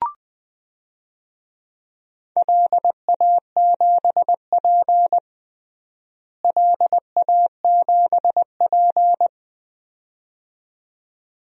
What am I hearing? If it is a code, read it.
Morse code, heard as a keyed tone, about 20 wpm, 700 Hz. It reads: LA7P LA7P
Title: DXCC-100 Call Signs - 4 Characters - Repeated 2x - 20wpm